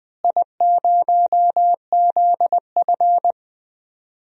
Morse code, heard as a keyed tone, about 20 wpm, 700 Hz. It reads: I0ZF
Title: DXCC-100 Call Signs - 4 Characters - Repeated 2x - 20wpm